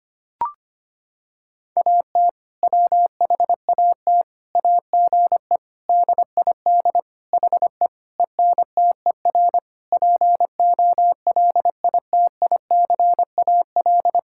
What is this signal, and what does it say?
Morse code, heard as a keyed tone, about 25 wpm, 700 Hz. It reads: AT WHAT AGE DID HE ENTER POLITICAL
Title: Sentences from Top 1000 Words - Review Part 5 - 25wpm